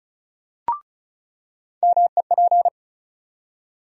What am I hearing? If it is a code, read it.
Morse code, heard as a keyed tone, about 34 wpm, 700 Hz. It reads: MEP
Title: USA Call Sign Postfixes - 3 Letters 35wpm